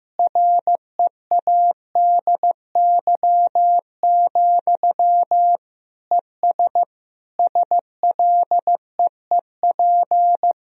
Morse code, heard as a keyed tone, about 15 wpm, 700 Hz. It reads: READY, ES SLEEP
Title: Sentences from Top 500 Words and Abbreviations 15wpm